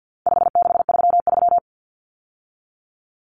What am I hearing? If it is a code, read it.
Morse code, heard as a keyed tone, about 50 wpm, 700 Hz. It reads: 5633
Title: Numbers - 4-Digit 50wpm